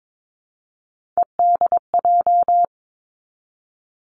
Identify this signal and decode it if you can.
Morse code, heard as a keyed tone, about 22 wpm, 700 Hz. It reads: EDJ
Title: Three Challenging Letter-Number Combinations 22wpm